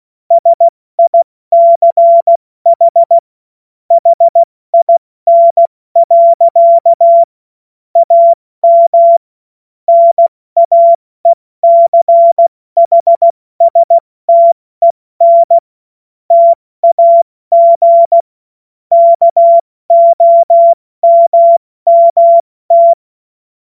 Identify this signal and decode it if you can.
Morse code, heard as a keyed tone, about 16 wpm, spaced as for 14 wpm, 680 Hz. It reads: SICH HIN. AM NAECHSTEN TAG KOMMT
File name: Morsekurs / Witze / Witz 11. Mai 2021, 70 F80